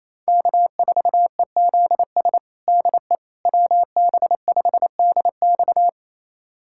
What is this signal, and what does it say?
Morse code, heard as a keyed tone, about 28 wpm, 700 Hz. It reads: K4EZS DE WB5DX
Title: USA Callsign DE Callsign - 5 Characters - 28wpm